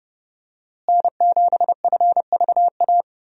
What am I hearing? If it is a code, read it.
Morse code, heard as a keyed tone, about 30 wpm, 700 Hz. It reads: N7FVA
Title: USA Call Signs - 5 Characters - Repeated 2x - 30wpm